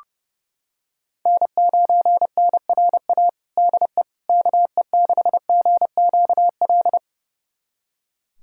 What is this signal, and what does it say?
Morse code, heard as a keyed tone, about 30 wpm, 700 Hz. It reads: N9NRA DE KE6GQL
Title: USA Callsign DE Callsign 30wpm